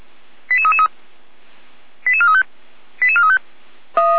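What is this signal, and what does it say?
Five-tone selective call (ZVEI), heard as quick runs of five tones, tones of about 70 ms. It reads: 88333 88435 88435
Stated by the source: ZVEI five-tone selective call received by SDR and recorded by the fire brigade (Feuerwehr) Orlamünde.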